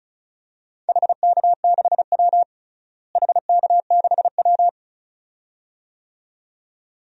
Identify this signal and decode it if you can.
Morse code, heard as a keyed tone, about 34 wpm, 700 Hz. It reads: HK6W HK6W
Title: DXCC-100 Call Signs - 4 Characters - Repeated 2x - 35wpm